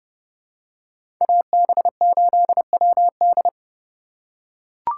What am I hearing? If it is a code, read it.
Morse code, heard as a keyed tone, about 30 wpm, 700 Hz. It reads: AB8WD
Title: USA Call Signs - 5 Characters - 2x Character Spacing - 30wpm